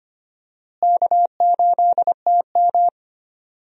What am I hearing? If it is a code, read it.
Morse code, heard as a keyed tone, about 25 wpm, 700 Hz. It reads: K8TM
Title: USA Call Signs - 4 Characters - Repeated 2x - 25wpm